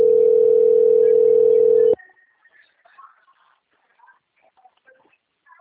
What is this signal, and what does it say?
Ringback tone (North America), 440 + 480 Hz.